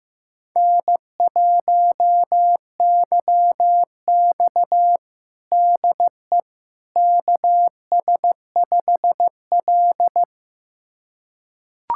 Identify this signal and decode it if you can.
Morse code, heard as a keyed tone, about 15 wpm, 700 Hz. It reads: N1YX DE KS5L
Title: USA Callsign DE Callsign - 4 Characters - 15wpm